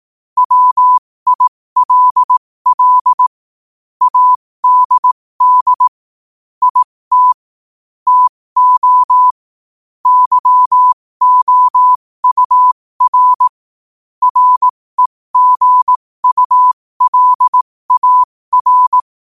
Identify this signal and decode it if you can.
Morse code, heard as a keyed tone, about 18 wpm, spaced as for 15 wpm, 1000 Hz. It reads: WILL ADD IT TO YOUR REGULAR
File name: qotdfiles / 15wpm